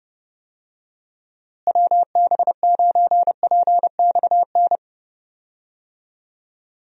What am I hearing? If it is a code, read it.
Morse code, heard as a keyed tone, about 29 wpm, 700 Hz. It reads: WB9PXN
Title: USA Call Signs - 6 Characters - Repeated 3x - 30wpm